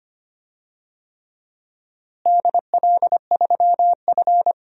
Morse code, heard as a keyed tone, about 25 wpm, 700 Hz. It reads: DL3F
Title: DXCC-100 Call Signs - 4 Characters - Repeated 3x - 25wpm